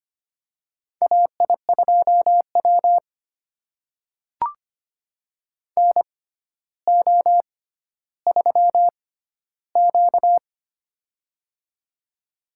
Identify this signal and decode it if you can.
Morse code, heard as a keyed tone, about 25 wpm, spaced as for 13 wpm, 700 Hz. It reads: AI2W NO3Q
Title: USA Call Signs - 4 Characters - 6x Character Spacing - 25wpm